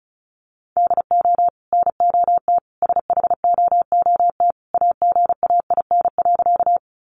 Morse code, heard as a keyed tone, about 34 wpm, 700 Hz. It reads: DO NOT SHOOT AGAIN.
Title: Sentences from Top 1200 Words 35wpm